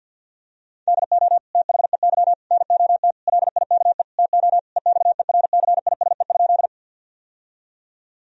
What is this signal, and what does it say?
Morse code, heard as a keyed tone, about 52 wpm, 700 Hz. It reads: DO THEY NOT LIKE TO EXERCISE?
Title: Sentences from Top 1000 Words - Review Part 4 - 50wpm